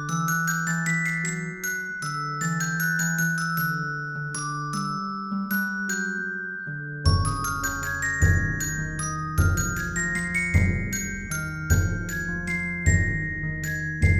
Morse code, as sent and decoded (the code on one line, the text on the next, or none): none